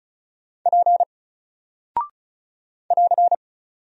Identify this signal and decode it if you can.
Morse code, heard as a keyed tone, about 34 wpm, 700 Hz. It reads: P <AR>
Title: Single Character (Letter:Number:Punctuation:Prosign) 35wpm